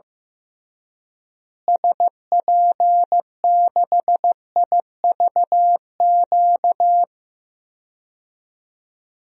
Morse code, heard as a keyed tone, about 15 wpm, 700 Hz. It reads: SP6IVQ
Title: DXCC-100 Call Signs - 6 Characters - Repeated 2x - 15wpm